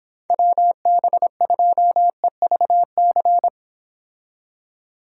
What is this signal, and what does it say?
Morse code, heard as a keyed tone, about 26 wpm, 700 Hz. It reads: WB2EVC